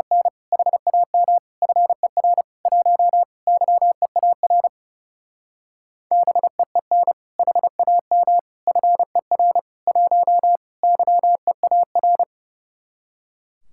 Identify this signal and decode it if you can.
Morse code, heard as a keyed tone, about 31 wpm, 700 Hz. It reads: N HAM FER 1 YEAR BEEN HAM FER 1 YEAR
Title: Speed-Racing Common QSO Phrases - 30wpm